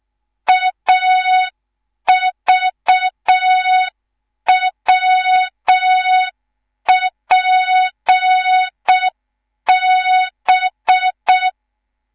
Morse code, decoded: AVWPB